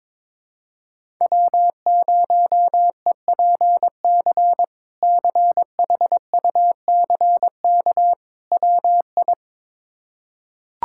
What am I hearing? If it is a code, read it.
Morse code, heard as a keyed tone, about 22 wpm, 700 Hz. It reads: W0EPC CHUCK WI
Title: K1USN SST QSO Practice - 22wpm